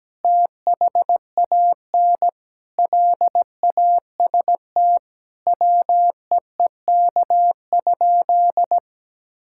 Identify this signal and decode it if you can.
Morse code, heard as a keyed tone, about 17 wpm, 700 Hz. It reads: THAN LAST WEEK?
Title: Sentences from Top 1000 Words - Review Part 5 - 17wpm